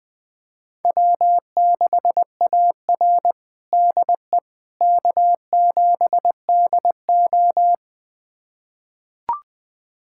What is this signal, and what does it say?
Morse code, heard as a keyed tone, about 21 wpm, 700 Hz. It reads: W6AR DE K7DO
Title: USA Callsign DE Callsign - 4 Characters - 20wpm